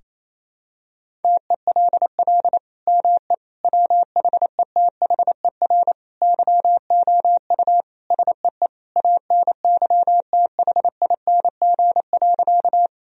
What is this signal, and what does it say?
Morse code, heard as a keyed tone, about 28 wpm, 700 Hz. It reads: TELL ME WHETHER YOU SEE ANYTHING.
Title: Sentences from Top 1000 Words - Review Part 3 - 28wpm